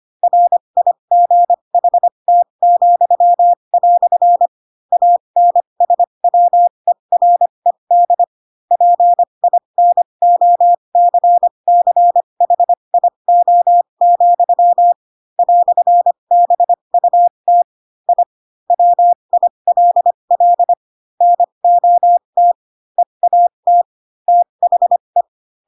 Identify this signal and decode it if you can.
Morse code, heard as a keyed tone, about 25 wpm, spaced as for 22 wpm, 700 Hz. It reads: RIGHT," ANSWERED PINOCCHIO, "BUT I WILL NOT EAT THE
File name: Pinocchio / Pinocchio ch7 18 25 ik1hge